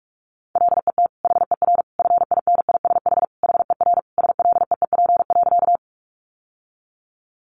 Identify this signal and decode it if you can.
Morse code, heard as a keyed tone, about 44 wpm, 700 Hz. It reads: LET HER FINISH HER SLEEP.
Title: Sentences from Top 1200 Words 45wpm